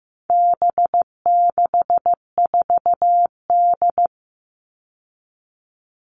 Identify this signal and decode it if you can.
Morse code, heard as a keyed tone, about 15 wpm, 700 Hz. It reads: B64D
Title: All Cut Numbers - 4-Digit 15wpm